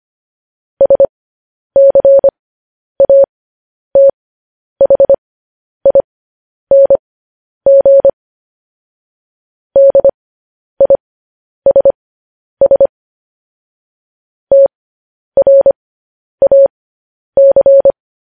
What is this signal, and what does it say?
Morse code, heard as a keyed tone, about 25 wpm, spaced as for 10 wpm, 550 Hz. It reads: SCATHING DISS TRAC
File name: podcast / News headlines-1x-25x10